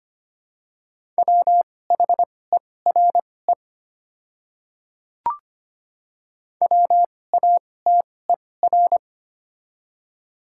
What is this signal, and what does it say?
Morse code, heard as a keyed tone, about 25 wpm, spaced as for 18 wpm, 690 Hz. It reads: WHERE WATER
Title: 2x Character Spacing - Top 200 Words - 25wpm